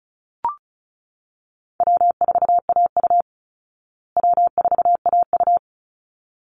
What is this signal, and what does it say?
Morse code, heard as a keyed tone, about 34 wpm, 700 Hz. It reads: W4AU W4AU
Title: Easy Call Signs Repeated 2x - 35wpm